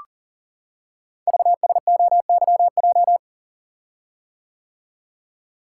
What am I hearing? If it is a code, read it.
Morse code, heard as a keyed tone, about 43 wpm, 700 Hz. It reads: VSOYJ
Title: Code Groups - 5 Letters - 40wpm